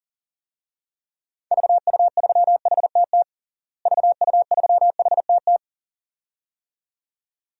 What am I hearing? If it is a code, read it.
Morse code, heard as a keyed tone, about 40 wpm, 700 Hz. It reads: VU3HTT VU3HTT
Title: DXCC-100 Call Signs - 6 Characters - Repeated 2x - 40wpm